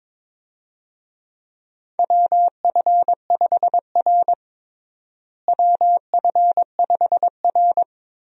Morse code, heard as a keyed tone, about 22 wpm, 700 Hz. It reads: WF5R WF5R
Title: USA Call Signs - 4 Characters - Repeated 3x - 22wpm